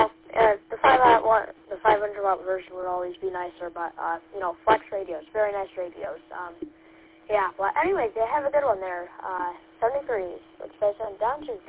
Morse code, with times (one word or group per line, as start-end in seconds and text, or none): none